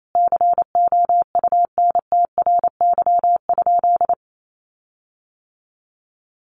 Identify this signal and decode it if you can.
Morse code, heard as a keyed tone, about 28 wpm, 700 Hz. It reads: COUNTRY?
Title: Sentences from Top 1400 Words 28wpm